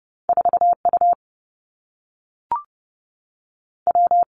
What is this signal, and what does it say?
Morse code, heard as a keyed tone, about 30 wpm, 700 Hz. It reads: 4 U W
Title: Two Challenging Letter-Number Combinations 30wpm